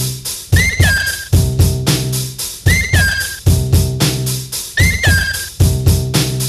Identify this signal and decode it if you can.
Emergency Alert System (SAME) tones (header bursts), heard as harsh buzzing bursts. It is heard from 0.6 to 5.5 s.